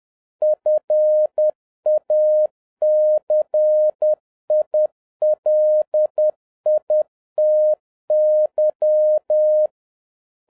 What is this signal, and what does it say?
Morse code, heard as a keyed tone, about 10 wpm, 610 Hz. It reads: FACILITY
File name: 2011-07-mp3 / n36 THE 3.11 2 10wpm